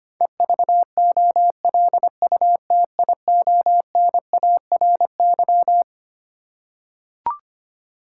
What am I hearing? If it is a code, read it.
Morse code, heard as a keyed tone, about 25 wpm, 700 Hz. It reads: EVOLUTIONARY
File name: Twelve Letter Words - 25wpm